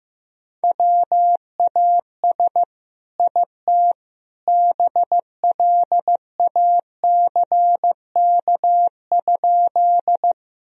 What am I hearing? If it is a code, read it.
Morse code, heard as a keyed tone, about 15 wpm, 700 Hz. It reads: WAS IT BLACK?
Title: Sentences from Top 1000 Words - Review Part 2 - 15wpm